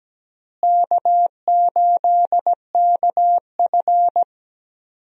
Morse code, heard as a keyed tone, about 17 wpm, 700 Hz. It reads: K8KF
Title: USA Call Signs - 4 Characters - Repeated 2x - 17wpm